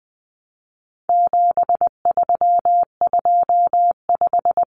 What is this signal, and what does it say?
Morse code, heard as a keyed tone, about 20 wpm, 700 Hz. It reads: 7325
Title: Numbers - 4-Digit 20wpm